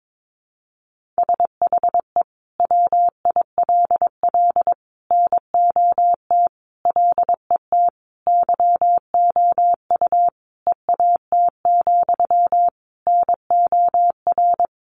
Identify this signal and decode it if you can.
Morse code, heard as a keyed tone, about 22 wpm, 700 Hz. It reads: SHE WILL NOT LET YOU EAT, NOR
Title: Sentences from Top 1500 Words 22wpm